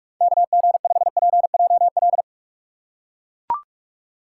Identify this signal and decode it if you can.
Morse code, heard as a keyed tone, about 42 wpm, 700 Hz. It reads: KG5PJL